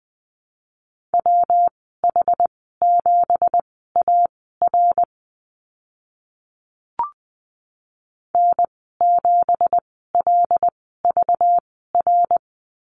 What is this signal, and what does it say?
Morse code, heard as a keyed tone, about 20 wpm, spaced as for 15 wpm, 700 Hz. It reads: WH7AR N7LVR
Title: 193 Lesson 31g - Letter v - Call Signs 20wpm